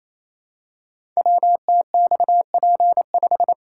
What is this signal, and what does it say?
Morse code, heard as a keyed tone, about 27 wpm, 700 Hz. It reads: WTXP5
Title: Code Groups - 5 Letter:Numbers - 28wpm